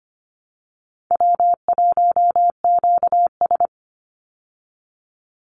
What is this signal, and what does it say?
Morse code, heard as a keyed tone, about 25 wpm, 700 Hz. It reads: W1QS
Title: Call Signs Repeated 3x - 25wpm